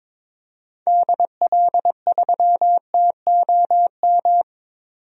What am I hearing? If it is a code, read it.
Morse code, heard as a keyed tone, about 22 wpm, 700 Hz. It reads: DL3TOM
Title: DXCC-100 Call Signs - 6 Characters - 22wpm